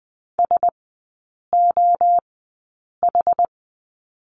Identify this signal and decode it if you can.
Morse code, heard as a keyed tone, about 20 wpm, spaced as for 8 wpm, 700 Hz. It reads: SOH
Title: 087 Lesson 15e - Number 5 - Three Characters 20wpm